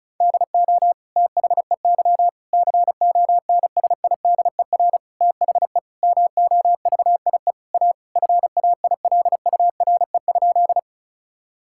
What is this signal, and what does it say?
Morse code, heard as a keyed tone, about 36 wpm, 700 Hz. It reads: DO THEY CONSIDER THE MOVIE A FAILURE?
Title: Sentences from Top 1000 Words - Review Part 6 - 35wpm